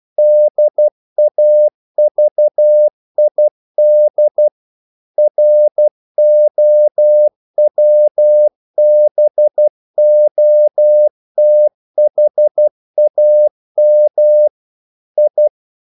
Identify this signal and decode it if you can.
Morse code, heard as a keyed tone, about 12 wpm, 600 Hz. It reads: DAVID ROWBOTHAM I
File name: mp3 files / the gardener - 12wpm